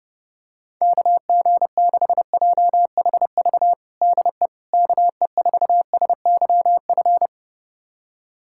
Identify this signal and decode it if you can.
Morse code, heard as a keyed tone, about 30 wpm, 700 Hz. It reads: KG6JHV DE KE4SYF
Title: USA Callsign DE Callsign - 6 Characters - 30wpm